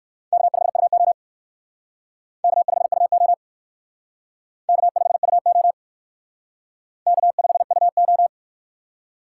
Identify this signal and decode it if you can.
Morse code, heard as a keyed tone, about 55 wpm, 700 Hz. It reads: K5UO K5UO K5UO K5UO
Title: Speed Racing - USA 1x2 Call Signs 45wpm